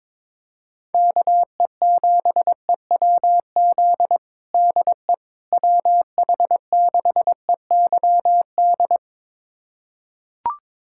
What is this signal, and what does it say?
Morse code, heard as a keyed tone, about 22 wpm, 700 Hz. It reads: KE7EWZ DE WH6EYD
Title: USA Callsign DE Callsign 22wpm